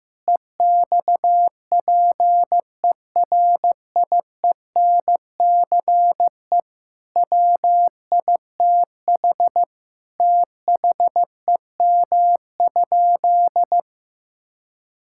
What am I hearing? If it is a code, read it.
Morse code, heard as a keyed tone, about 15 wpm, 700 Hz. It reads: EXPERIENCE WITH THEM?
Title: Sentences from Top 1000 Words - Review Part 5 - 15wpm